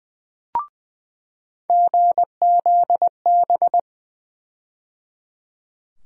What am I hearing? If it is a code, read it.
Morse code, heard as a keyed tone, about 20 wpm, 700 Hz. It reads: GZB
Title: USA Call Sign Postfixes - 3 Letters 20wpm